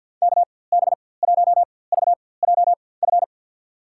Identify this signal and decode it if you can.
Morse code, heard as a keyed tone, about 53 wpm, spaced as for 24 wpm, 700 Hz. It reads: KB1VJF
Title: USA Call Signs - 6 Characters - 4x Character Spacing - 50wpm